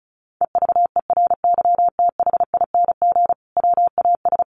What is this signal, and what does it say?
Morse code, heard as a keyed tone, about 34 wpm, 700 Hz. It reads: EVERYTHING WAS
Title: Sentences from Top 900 Words 35wpm